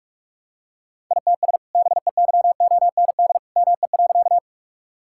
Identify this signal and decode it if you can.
Morse code, heard as a keyed tone, about 43 wpm, 700 Hz. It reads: ITS BEYOND ME.
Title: Sentences from Top 1000 Words - Review Part 6 - 45wpm